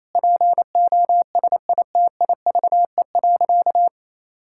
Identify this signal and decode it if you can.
Morse code, heard as a keyed tone, about 28 wpm, 700 Hz. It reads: POSITIVE.
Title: Sentences from Top 1000 Words - Review Part 5 - 28wpm